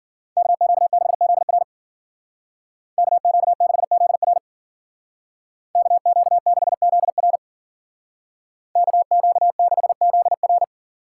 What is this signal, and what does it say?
Morse code, heard as a keyed tone, about 52 wpm, 700 Hz. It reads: KQ6ZR KQ6ZR KQ6ZR KQ6ZR